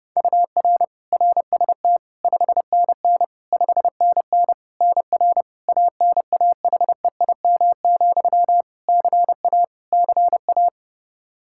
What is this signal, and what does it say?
Morse code, heard as a keyed tone, about 30 wpm, 700 Hz. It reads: UR RST 5NN 5NN NR ANAHEIM, CA CA
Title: Practice QSO - Exchange 1 - Beginning - 30wpm